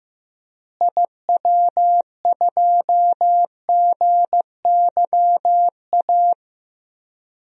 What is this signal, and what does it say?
Morse code, heard as a keyed tone, about 15 wpm, 700 Hz. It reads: IW2GYA
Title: DXCC-100 Call Signs - 6 Characters - Repeated 2x - 15wpm